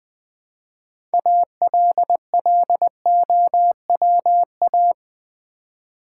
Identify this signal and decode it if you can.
Morse code, heard as a keyed tone, about 20 wpm, 700 Hz. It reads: ALLOWA
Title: Build A Word - 3 Letters At A Time - 9-Letter Words - 20wpm